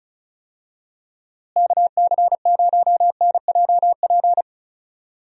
Morse code, heard as a keyed tone, about 34 wpm, 700 Hz. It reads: KC0NJP
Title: USA Call Signs - 6 Characters - Repeated 2x - 35wpm